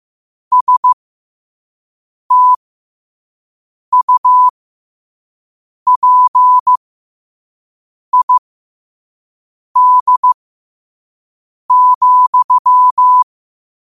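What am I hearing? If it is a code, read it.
Morse code, heard as a keyed tone, about 15 wpm, spaced as for 5 wpm, 1000 Hz. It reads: STUPID,
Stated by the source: Morse code at 5 wpm.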